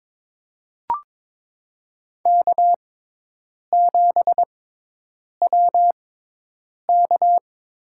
Morse code, heard as a keyed tone, about 21 wpm, spaced as for 8 wpm, 700 Hz. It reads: K7WK
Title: USA Call Signs - 6x Character Spacing - 22wpm